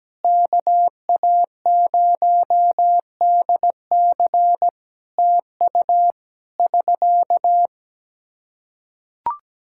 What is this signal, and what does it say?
Morse code, heard as a keyed tone, about 17 wpm, 700 Hz. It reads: KA0DC TU <SK>
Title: Practice QSO - Exchange 4 - Ending - 17wpm